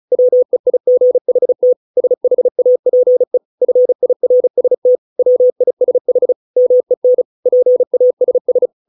Morse code, heard as a keyed tone, about 34 wpm, 500 Hz. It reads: WEIGHT SHAPE FIRST WISH MEN PASS